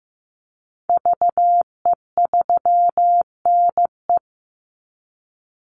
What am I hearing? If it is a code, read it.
Morse code, heard as a keyed tone, about 15 wpm, 700 Hz. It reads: VE3NE
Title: Call Signs Repeated 2x - 15wpm